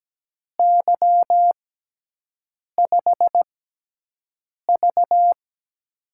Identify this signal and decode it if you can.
Morse code, heard as a keyed tone, about 17 wpm, spaced as for 6 wpm, 700 Hz. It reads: Y5V
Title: USA Call Signs - 4 Characters - 6x Character Spacing - 17wpm